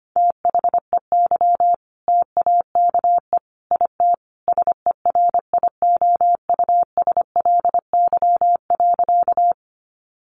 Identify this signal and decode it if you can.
Morse code, heard as a keyed tone, about 25 wpm, 700 Hz. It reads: THEY TAKE IT SERIOUSLY.
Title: Sentences from Top 1200 Words 25wpm